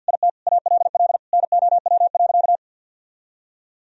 Morse code, heard as a keyed tone, about 51 wpm, 700 Hz. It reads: IT ALL NOW.
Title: Sentences from Top 1000 Words - Review Part 1 - 50wpm